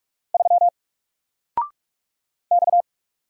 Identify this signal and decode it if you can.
Morse code, heard as a keyed tone, about 44 wpm, 700 Hz. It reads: 3 X
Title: Single Letter-Number 45wpm